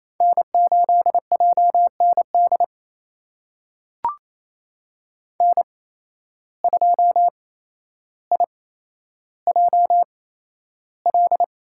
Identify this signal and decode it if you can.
Morse code, heard as a keyed tone, about 28 wpm, spaced as for 12 wpm, 700 Hz. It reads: N8JND N2IJL
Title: USA Call Signs - 5 Characters - 8x Character Spacing - 28wpm